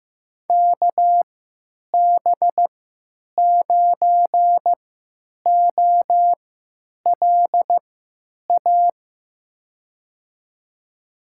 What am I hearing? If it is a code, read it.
Morse code, heard as a keyed tone, about 15 wpm, spaced as for 9 wpm, 700 Hz. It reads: KB9OLA
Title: USA Call Signs - 3x Character Spacing - 15wpm